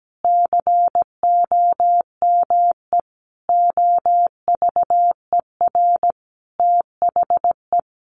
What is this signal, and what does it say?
Morse code, heard as a keyed tone, about 17 wpm, 700 Hz. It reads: COME OVER THE
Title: Sentences from Top 300 Words 17wpm